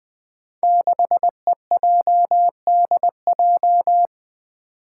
Morse code, heard as a keyed tone, about 20 wpm, 700 Hz. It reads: 6EJDJ
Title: Five Challenging Letter-Number Combinations 20wpm